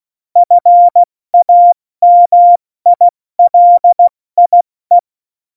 Morse code, heard as a keyed tone, about 16 wpm, 700 Hz. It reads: FAMILIE
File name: Morsekurs / Witze / Witz 06 March 2018, 70 F80